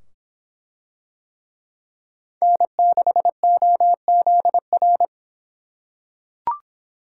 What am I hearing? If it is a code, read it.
Morse code, heard as a keyed tone, about 24 wpm, 700 Hz. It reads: N6OZR